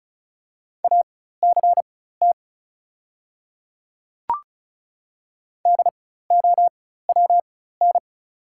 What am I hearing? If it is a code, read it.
Morse code, heard as a keyed tone, about 36 wpm, spaced as for 16 wpm, 700 Hz. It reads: ACT DOWN
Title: 4x Character Spacing - Top 200 Words - 35wpm